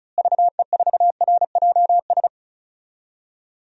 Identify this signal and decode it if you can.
Morse code, heard as a keyed tone, about 36 wpm, 700 Hz. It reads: VE4RJS
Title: DXCC-100 Call Signs - 6 Characters - Repeated 3x - 35wpm